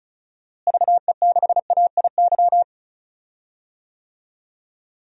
Morse code, heard as a keyed tone, about 36 wpm, 700 Hz. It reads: VE6AIY
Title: DXCC-100 Call Signs - 6 Characters - Repeated 3x - 35wpm